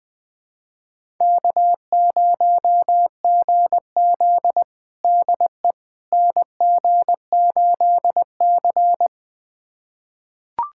Morse code, heard as a keyed tone, about 21 wpm, 700 Hz. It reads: K0GZ DE NG8C
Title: USA Callsign DE Callsign - 4 Characters - 20wpm